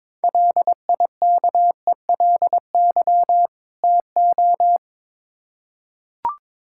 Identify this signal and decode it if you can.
Morse code, heard as a keyed tone, about 21 wpm, 700 Hz. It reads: LIKELY TO
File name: Build A Phrase - 2-Word Phrases - 22wpm